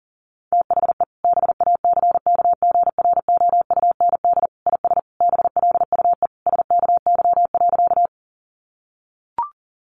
Morse code, heard as a keyed tone, about 40 wpm, 700 Hz. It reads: THE BACKGROUND IS BLUE SKY.
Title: Sentences from Top 1300 Words 40wpm